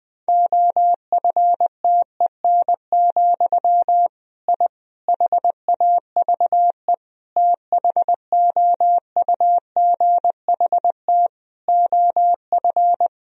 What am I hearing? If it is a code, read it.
Morse code, heard as a keyed tone, about 20 wpm, 700 Hz. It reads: OFTEN, I HAVE THOUGHT OF
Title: Sentences from Top 1000 Words - Review Part 1 - 20wpm